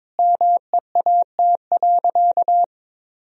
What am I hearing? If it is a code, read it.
Morse code, heard as a keyed tone, about 22 wpm, 700 Hz. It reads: MEAT.
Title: Sentences from Top 1000 Words - Review Part 3 - 22wpm